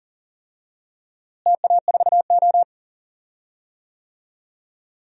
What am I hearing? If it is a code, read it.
Morse code, heard as a keyed tone, about 40 wpm, 700 Hz. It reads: TA4O
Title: DXCC-100 Call Signs - 4 Characters - Repeated 3x - 40wpm